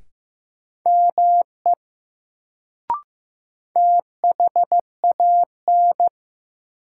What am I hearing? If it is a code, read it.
Morse code, heard as a keyed tone, about 15 wpm, 700 Hz. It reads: ME THAN
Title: Sets of 2 Words - New Word Drill - 15wpm